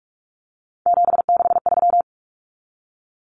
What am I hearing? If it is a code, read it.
Morse code, heard as a keyed tone, about 44 wpm, 700 Hz. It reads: 763
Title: Numbers - 3-Digit 45wpm